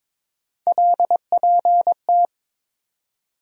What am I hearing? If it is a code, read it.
Morse code, heard as a keyed tone, about 21 wpm, 700 Hz. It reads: LPT
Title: USA Call Sign Postfixes - 3 Letters 22wpm